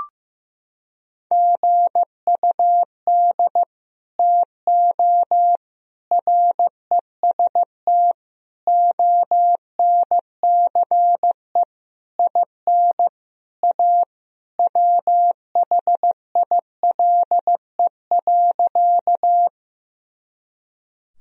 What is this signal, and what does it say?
Morse code, heard as a keyed tone, about 15 wpm, 700 Hz. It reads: GUD TO REST ONCE IN A WHILE.
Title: Sentences from Top 500 Words and Abbreviations 15wpm